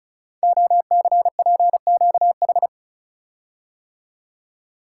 Morse code, heard as a keyed tone, about 36 wpm, 700 Hz. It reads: OCPQH
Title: Code Groups - 5 Letter:Numbers - 35wpm